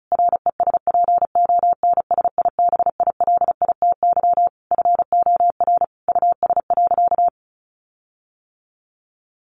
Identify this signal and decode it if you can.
Morse code, heard as a keyed tone, about 34 wpm, 700 Hz. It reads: RESPONSIBILITY FOR US.
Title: Sentences from Top 1500 Words 35wpm